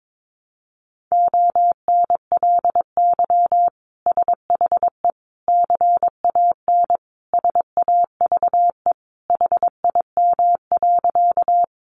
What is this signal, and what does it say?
Morse code, heard as a keyed tone, about 22 wpm, 700 Hz. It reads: ONLY SHE CAN SAVE HIM.
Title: Sentences from Top 1100 Words 22wpm